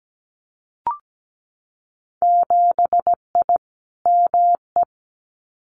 Morse code, heard as a keyed tone, about 17 wpm, 700 Hz. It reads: 7I ME
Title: Winter Field Day Exchanges Encore 17wpm